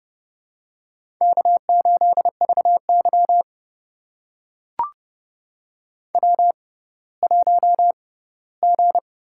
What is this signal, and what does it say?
Morse code, heard as a keyed tone, about 30 wpm, spaced as for 17 wpm, 700 Hz. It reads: K8VY W1G
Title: USA Call Signs - 4 Characters - 6x Character Spacing - 30wpm